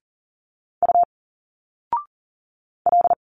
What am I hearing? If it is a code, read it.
Morse code, heard as a keyed tone, about 40 wpm, 700 Hz. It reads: U L
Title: Single Letters 40wpm